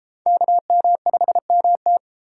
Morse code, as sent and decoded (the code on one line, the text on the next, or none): -.- -- ..... -- -
KM5MT